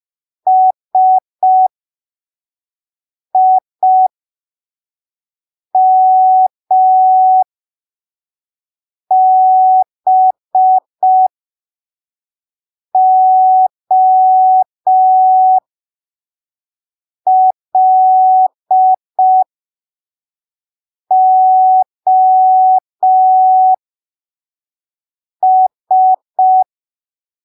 Morse code, heard as a keyed tone, about 5 wpm, spaced as for 3 wpm, 750 Hz. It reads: SIMBOLOS
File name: arquivos / texto / texto 18 05ppm